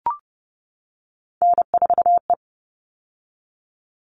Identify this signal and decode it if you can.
Morse code, heard as a keyed tone, about 30 wpm, 700 Hz. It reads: N4E
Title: All Cut Numbers - 3-Digit 30wpm